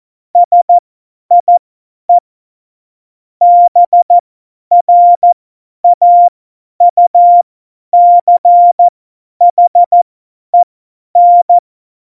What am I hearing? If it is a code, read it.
Morse code, heard as a keyed tone, about 14 wpm, spaced as for 10 wpm, 700 Hz. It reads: SIE BRAUCHEN